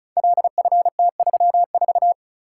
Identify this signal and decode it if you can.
Morse code, heard as a keyed tone, about 36 wpm, 700 Hz. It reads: LFT34
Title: Code Groups - 5 Letter:Numbers - 35wpm